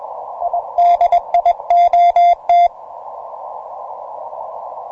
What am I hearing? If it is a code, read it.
Morse code, heard as a keyed tone, about 21 wpm, 700 Hz. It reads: DIOT
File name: simple-file-list / T30GC 20MCW